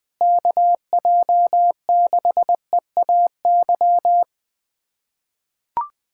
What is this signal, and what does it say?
Morse code, heard as a keyed tone, about 20 wpm, 700 Hz. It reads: KJ6EAY
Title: USA Callsign DE Callsign - 6 Characters - 20wpm